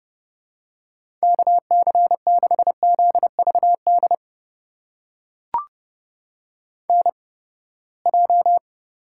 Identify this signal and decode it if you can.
Morse code, heard as a keyed tone, about 30 wpm, spaced as for 21 wpm, 700 Hz. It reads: KC6ZVD NJ
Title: USA Call Signs - 8x Character Spacing - 30wpm